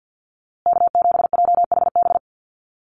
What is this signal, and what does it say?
Morse code, heard as a keyed tone, about 50 wpm, 700 Hz. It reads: K7JHB
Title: USA Call Signs Repeated 2x 50wpm